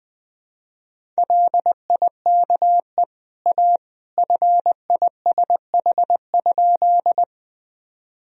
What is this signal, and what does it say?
Morse code, heard as a keyed tone, about 20 wpm, 700 Hz. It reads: LIKE A FISH?
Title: Sentences from Top 1000 Words - Review Part 1 - 20wpm